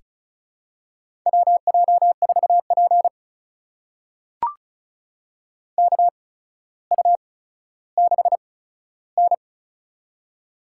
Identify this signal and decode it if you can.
Morse code, heard as a keyed tone, about 36 wpm, spaced as for 15 wpm, 700 Hz. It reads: WJ4P KU6N
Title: USA Call Signs - 4 Characters - 8x Character Spacing - 35wpm